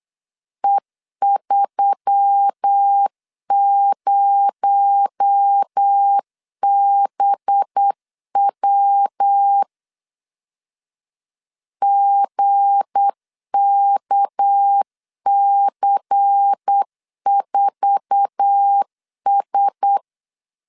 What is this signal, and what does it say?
Morse code, heard as a keyed tone, about 8 wpm, 800 Hz. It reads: E30BW GKC4S
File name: LEZIONI / 9 2 8wpm 3437